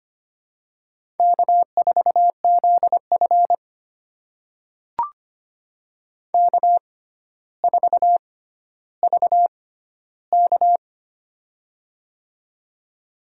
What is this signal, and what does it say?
Morse code, heard as a keyed tone, about 25 wpm, spaced as for 13 wpm, 700 Hz. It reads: K4ZF K4VK
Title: USA Call Signs - 4 Characters - 6x Character Spacing - 25wpm